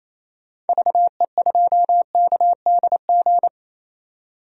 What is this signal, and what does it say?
Morse code, heard as a keyed tone, about 28 wpm, 700 Hz. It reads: VE2KDG
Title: DXCC-100 Call Signs - 6 Characters - 28wpm